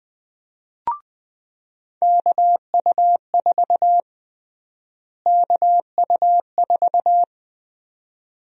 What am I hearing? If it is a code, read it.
Morse code, heard as a keyed tone, about 20 wpm, 700 Hz. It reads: KU4 KU4
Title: USA Call Sign Prefixes - Repeated 2x - 20wpm